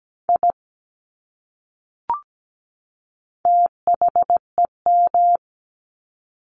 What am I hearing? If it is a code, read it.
Morse code, heard as a keyed tone, about 17 wpm, 700 Hz. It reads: I THEM
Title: Top 200 Words 17wpm